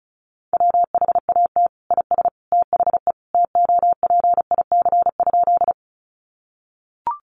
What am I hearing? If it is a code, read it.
Morse code, heard as a keyed tone, about 34 wpm, 700 Hz. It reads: WHAT IS THE TOPIC?
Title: Sentences from Top 1200 Words 35wpm